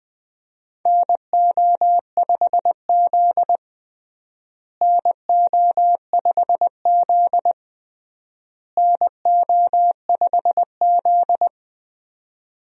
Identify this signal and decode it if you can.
Morse code, heard as a keyed tone, about 21 wpm, 700 Hz. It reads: NO5Z NO5Z NO5Z
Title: USA Call Signs - 4 Characters - Repeated 3x - 20wpm